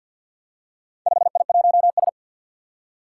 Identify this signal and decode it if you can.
Morse code, heard as a keyed tone, about 46 wpm, 700 Hz. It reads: 5I1S
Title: Four Challenging Letter-Number Combinations 50wpm